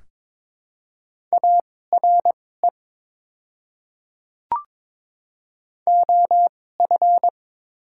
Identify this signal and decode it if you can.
Morse code, heard as a keyed tone, about 22 wpm, spaced as for 16 wpm, 690 Hz. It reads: ARE OF
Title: 2x Character Spacing - Top 100 Words - 22wpm